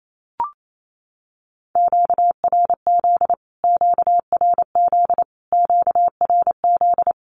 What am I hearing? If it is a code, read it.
Morse code, heard as a keyed tone, about 28 wpm, 700 Hz. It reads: QRZ QRZ QRZ
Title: Q-Codes Repeated 3x 28wpm